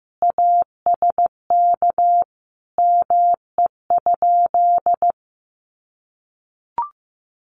Morse code, heard as a keyed tone, about 15 wpm, 700 Hz. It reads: ASK ME?
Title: Sentences from Top 200 Words 15wpm